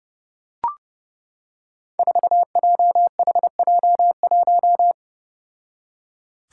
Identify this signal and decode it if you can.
Morse code, heard as a keyed tone, about 32 wpm, 700 Hz. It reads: 4JHJ1
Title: Five Challenging Letter-Number Combinations 30wpm